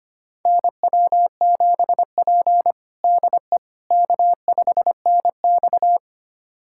Morse code, heard as a keyed tone, about 25 wpm, 700 Hz. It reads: NW7P DE K5NX
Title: USA Callsign DE Callsign - 4 Characters - 25wpm